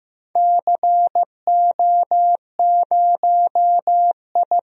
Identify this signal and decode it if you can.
Morse code, heard as a keyed tone, about 15 wpm, 700 Hz. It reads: CO0I
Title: DXCC-100 Call Signs - 4 Characters - 15wpm